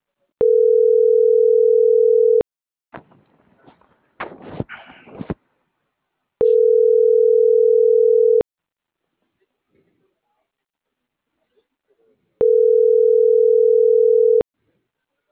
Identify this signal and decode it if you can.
Ringback tone (North America), 440 + 480 Hz.